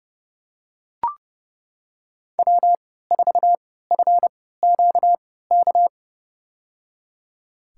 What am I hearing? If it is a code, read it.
Morse code, heard as a keyed tone, about 30 wpm, spaced as for 17 wpm, 700 Hz. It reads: W4FQK
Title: USA Call Signs - 5 Characters - 3x Character Spacing - 30wpm